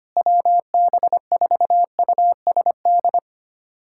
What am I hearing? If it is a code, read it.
Morse code, heard as a keyed tone, about 25 wpm, 700 Hz. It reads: WB4USD
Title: USA Callsign DE Callsign - 6 Characters - 25wpm